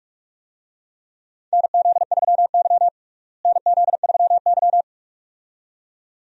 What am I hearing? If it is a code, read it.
Morse code, heard as a keyed tone, about 41 wpm, 700 Hz. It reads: NZ3Y NZ3Y
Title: USA Call Signs - 4 Characters - Repeated 2x - 45wpm